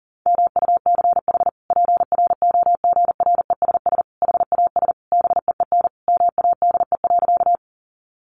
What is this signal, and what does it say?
Morse code, heard as a keyed tone, about 40 wpm, 700 Hz. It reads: MUCH PROGRESS HAS BEEN MADE.
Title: Sentences from Top 1400 Words 40wpm